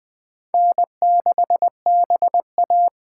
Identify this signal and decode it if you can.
Morse code, heard as a keyed tone, about 20 wpm, 700 Hz. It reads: N6BA
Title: USA Call Signs - 4 Characters - 4x Character Spacing - 20wpm